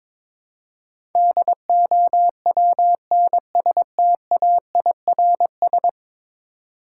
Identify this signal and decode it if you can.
Morse code, heard as a keyed tone, about 22 wpm, 700 Hz. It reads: DOWNSTAIRS
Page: morse-code-ninja-practice-22wpm